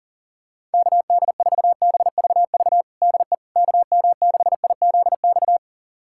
Morse code, heard as a keyed tone, about 40 wpm, 700 Hz. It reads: KD4BVV DE KM6IZX
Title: USA Callsign DE Callsign - 6 Characters - 40wpm